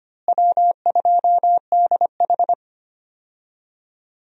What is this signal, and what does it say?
Morse code, heard as a keyed tone, about 25 wpm, 700 Hz. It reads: W2DH
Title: USA Call Signs - 4 Characters - Repeated 3x - 25wpm